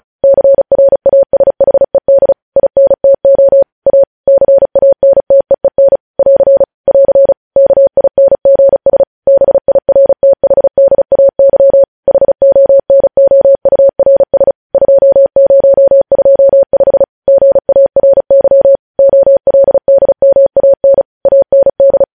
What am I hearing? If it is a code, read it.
Morse code, heard as a keyed tone, about 36 wpm, 550 Hz. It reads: CRASHED INTO A CANTEEN <AR> <AR> KINGS BIRTHDAY HONOURS 2025 GARY OLDMAN AND
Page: news headlines in morse code at 35 wpm